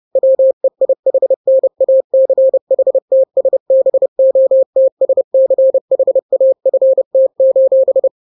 Morse code, heard as a keyed tone, about 30 wpm, 530 Hz. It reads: WEIHNACHTSBOTSCHAFT:
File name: daily / 29wpm